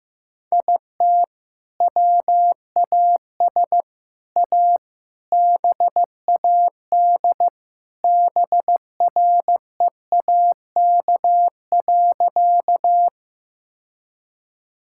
Morse code, heard as a keyed tone, about 15 wpm, 700 Hz. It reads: IT WAS A BAD BREAK.
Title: Sentences from Top 1000 Words - Review Part 6 - 15wpm